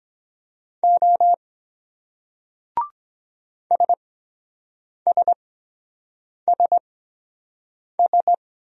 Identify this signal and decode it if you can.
Morse code, heard as a keyed tone, about 23 wpm, spaced as for 7 wpm, 700 Hz. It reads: O SSSS